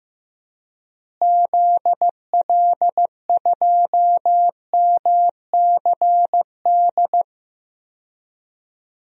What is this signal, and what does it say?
Morse code, heard as a keyed tone, about 15 wpm, 700 Hz. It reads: ZL2MCD
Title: DXCC-100 Call Signs - 6 Characters - Repeated 2x - 15wpm